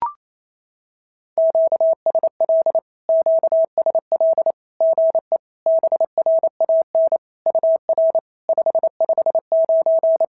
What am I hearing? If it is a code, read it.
Morse code, heard as a keyed tone, about 28 wpm, 650 Hz. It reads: QSL QSL GE BRAN UR 559
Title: SOTA QSO Practice 28wpm